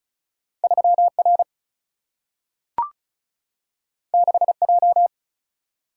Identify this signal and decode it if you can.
Morse code, heard as a keyed tone, about 36 wpm, 700 Hz. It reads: 3 R 6 J
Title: Single-Letter Single-Number 35wpm